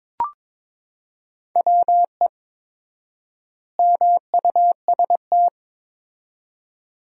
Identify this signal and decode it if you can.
Morse code, heard as a keyed tone, about 22 wpm, 700 Hz. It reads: WE MUST
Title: 4x Word-Spacing - Sentences from Top 300 Words - 22wpm